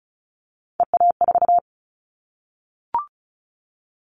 Morse code, heard as a keyed tone, about 34 wpm, 700 Hz. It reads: EA4
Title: Callsign Prefixes for Top 25 DXCC - Repeated 3x - 35wpm